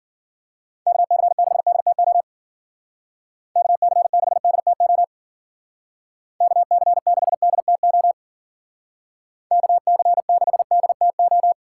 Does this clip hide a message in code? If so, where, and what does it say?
Yes: Morse code from 0.9 to 11.5 s, heard as a keyed tone, about 53 wpm, 700 Hz. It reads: KC6DTO KC6DTO KC6DTO KC6DTO